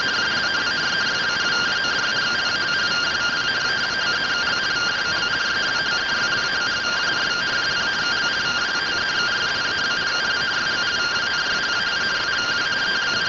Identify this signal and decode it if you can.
Two-tone teleprinter signal, about 170 Hz shift.
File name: findings / CT AXIAL -572mm 2.gen-11 rtty rx